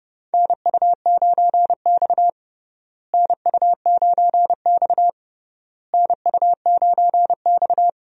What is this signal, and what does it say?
Morse code, heard as a keyed tone, about 30 wpm, 700 Hz. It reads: NU9X NU9X NU9X
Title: USA Call Signs - 4 Characters - Repeated 3x - 30wpm